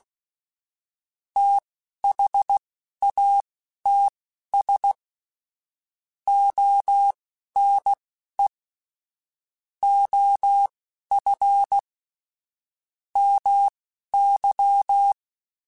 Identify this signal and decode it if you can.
Morse code, heard as a keyed tone, about 16 wpm, spaced as for 11 wpm, 800 Hz. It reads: THATS ONE OF MY
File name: (spaced) / QSOs Spaced-mp3s / WD0FBT 1 @15 sp